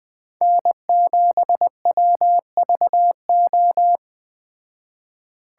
Morse code, heard as a keyed tone, about 20 wpm, 700 Hz. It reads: N7WVO
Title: Speed Racing - USA 1x3 Call Signs 17wpm